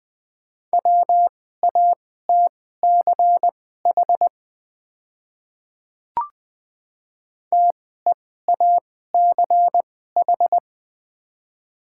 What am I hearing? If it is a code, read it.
Morse code, heard as a keyed tone, about 20 wpm, spaced as for 14 wpm, 700 Hz. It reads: WATCH TEACH
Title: 2x Character Spacing - Top 500 Words - 20wpm